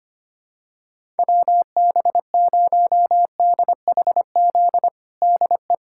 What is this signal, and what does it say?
Morse code, heard as a keyed tone, about 25 wpm, 700 Hz. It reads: WB0DHZ DE
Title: USA Callsign DE Callsign - 6 Characters - 25wpm